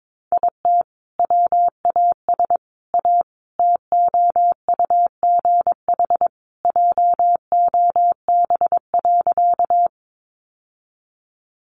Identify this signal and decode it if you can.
Morse code, heard as a keyed tone, about 22 wpm, 700 Hz. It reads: IT WAS A TOUGH JOB.
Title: Sentences from Top 1200 Words 22wpm